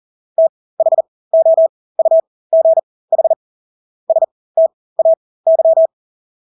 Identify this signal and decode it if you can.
Morse code, heard as a keyed tone, about 40 wpm, spaced as for 20 wpm, 650 Hz. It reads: THOUGH STAY
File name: Engelske ord 40-20 wpm